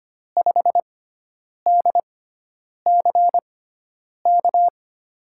Morse code, heard as a keyed tone, about 25 wpm, spaced as for 9 wpm, 700 Hz. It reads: 5DCK
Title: USA Call Signs - 6x Character Spacing - 25wpm